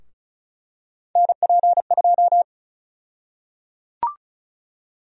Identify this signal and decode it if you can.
Morse code, heard as a keyed tone, about 36 wpm, 700 Hz. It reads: NP2
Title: USA Call Sign Prefixes - 35wpm